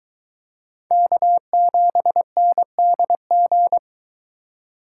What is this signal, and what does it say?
Morse code, heard as a keyed tone, about 22 wpm, 700 Hz. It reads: K7NDG